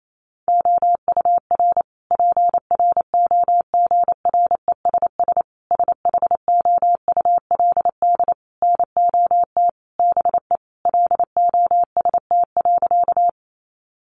Morse code, heard as a keyed tone, about 28 wpm, 700 Hz. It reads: OUR PROGRESS SHOULD NOT BE LOST.
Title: Sentences from Top 1400 Words 28wpm